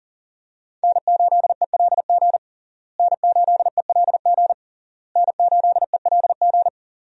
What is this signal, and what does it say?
Morse code, heard as a keyed tone, about 40 wpm, 700 Hz. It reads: N8ELG N8ELG N8ELG
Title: USA Call Signs - 5 Characters - Repeated 3x - 40wpm